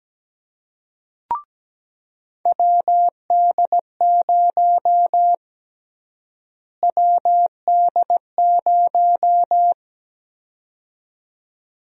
Morse code, heard as a keyed tone, about 17 wpm, 700 Hz. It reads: WD0 WD0
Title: USA Call Sign Prefixes - Repeated 2x - 17wpm